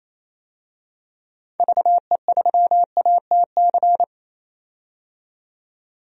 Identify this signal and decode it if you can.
Morse code, heard as a keyed tone, about 28 wpm, 700 Hz. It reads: VE3ATC
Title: DXCC-100 Call Signs - 6 Characters - 28wpm